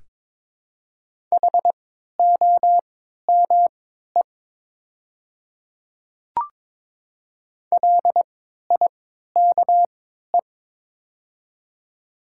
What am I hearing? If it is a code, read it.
Morse code, heard as a keyed tone, about 21 wpm, spaced as for 12 wpm, 700 Hz. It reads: HOME LIKE
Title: 3x Character Spacing - Top 200 Words - 22wpm